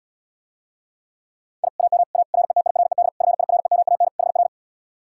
Morse code, heard as a keyed tone, about 77 wpm, 700 Hz. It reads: I AM A RETIRED FIREMAN HR